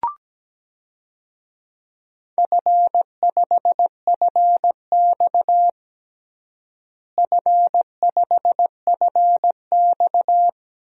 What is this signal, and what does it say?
Morse code, heard as a keyed tone, about 17 wpm, 700 Hz. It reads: F5FX F5FX
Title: DXCC-100 Call Signs - 4 Characters - Repeated 3x - 17wpm